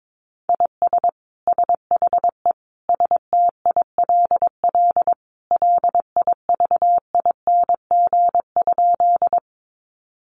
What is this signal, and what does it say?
Morse code, heard as a keyed tone, about 22 wpm, 700 Hz. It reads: IS SHE STILL LIVING?
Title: Sentences from Top 1100 Words 22wpm